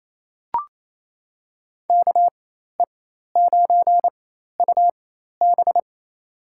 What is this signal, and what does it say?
Morse code, heard as a keyed tone, about 28 wpm, spaced as for 13 wpm, 700 Hz. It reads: KE9UB
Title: USA Call Signs - 5 Characters - 4x Character Spacing - 28wpm